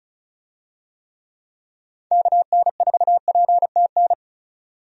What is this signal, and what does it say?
Morse code, heard as a keyed tone, about 36 wpm, 700 Hz. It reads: KN4PTN
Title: USA Call Signs - 6 Characters - 35wpm